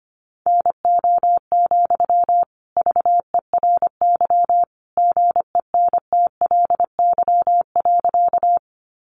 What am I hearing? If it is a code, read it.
Morse code, heard as a keyed tone, about 25 wpm, 700 Hz. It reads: NO, VERY GENTLY.
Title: Sentences from Top 1500 Words 25wpm